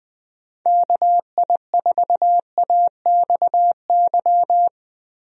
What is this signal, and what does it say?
Morse code, heard as a keyed tone, about 20 wpm, 700 Hz. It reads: KI4AXY
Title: USA Call Signs - 6 Characters - 20wpm